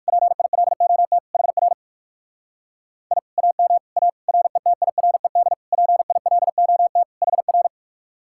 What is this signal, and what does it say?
Morse code, heard as a keyed tone, about 59 wpm, 700 Hz. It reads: PILOT HR I AM A RETIRED PILOT HR